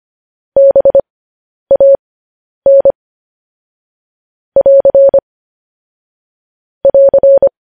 Morse code, heard as a keyed tone, about 25 wpm, spaced as for 10 wpm, 550 Hz. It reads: BAN <AR> <AR>
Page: news headlines in morse code at 10 wpm